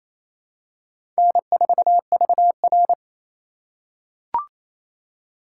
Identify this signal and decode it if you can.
Morse code, heard as a keyed tone, about 28 wpm, 700 Hz. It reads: N4VR
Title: USA Call Signs - 4 Characters - 3x Character Spacing - 28wpm